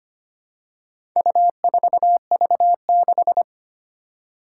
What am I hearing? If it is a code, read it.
Morse code, heard as a keyed tone, about 25 wpm, 700 Hz. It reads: U4V6
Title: Four Challenging Letter-Number Combinations 25wpm